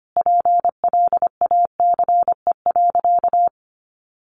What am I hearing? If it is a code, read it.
Morse code, heard as a keyed tone, about 25 wpm, 700 Hz. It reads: PLACE.
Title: Sentences from Top 1200 Words 25wpm